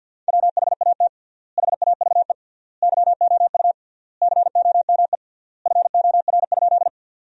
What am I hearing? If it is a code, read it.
Morse code, heard as a keyed tone, about 50 wpm, 700 Hz. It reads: WHAT HAVE YOU COME FOR?
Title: 3x Word-Spacing - Sentences from Top 100 Words - 50wpm